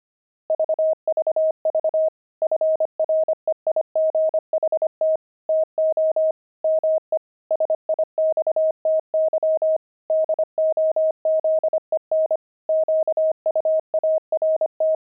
Morse code, heard as a keyed tone, about 25 wpm, 630 Hz. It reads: VVV FREIGHT TO ME SIXTY DOZEN QUART
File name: pangram-fs-25-cs-25-twelve